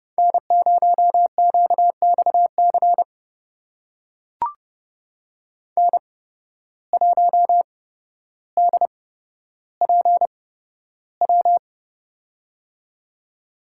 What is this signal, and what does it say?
Morse code, heard as a keyed tone, about 30 wpm, spaced as for 13 wpm, 700 Hz. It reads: N0QXC N1DPW
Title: USA Call Signs - 5 Characters - 8x Character Spacing - 30wpm